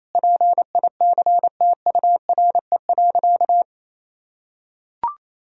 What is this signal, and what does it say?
Morse code, heard as a keyed tone, about 28 wpm, 700 Hz. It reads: PICTURE.
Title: Sentences from Top 1000 Words - Review Part 1 - 28wpm